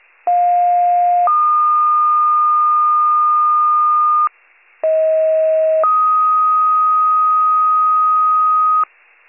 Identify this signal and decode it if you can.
Two-tone page, heard as a one-second tone, then a three-second tone, about 710 then 1220 Hz, about 640 then 1250 Hz. It is heard from 0.3 to 8.8 s.